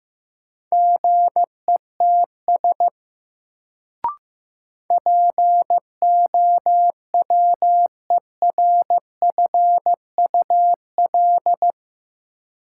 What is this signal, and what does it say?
Morse code, heard as a keyed tone, about 15 wpm, 700 Hz. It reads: GETS POWERFUL
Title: Sentences from Top 1300 Words - New Word Drill Alt - 15wpm